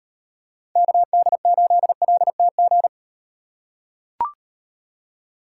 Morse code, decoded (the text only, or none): KD8LTG